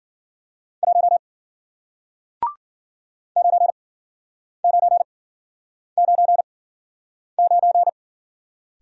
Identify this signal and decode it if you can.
Morse code, heard as a keyed tone, about 53 wpm, spaced as for 9 wpm, 700 Hz. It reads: 1 9999